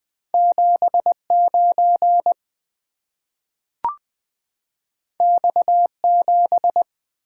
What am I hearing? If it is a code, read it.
Morse code, heard as a keyed tone, about 20 wpm, 700 Hz. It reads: 7 9 X 7
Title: Two Letter-Number Combinations - 20wpm